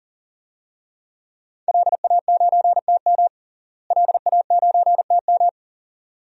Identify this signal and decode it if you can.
Morse code, heard as a keyed tone, about 40 wpm, 700 Hz. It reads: LA9TM LA9TM
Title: DXCC-100 Call Signs - 5 Characters - Repeated 2x - 40wpm